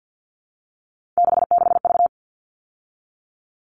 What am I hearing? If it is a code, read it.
Morse code, heard as a keyed tone, about 50 wpm, 700 Hz. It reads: 66V
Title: All Cut Numbers - 3-Digit 50wpm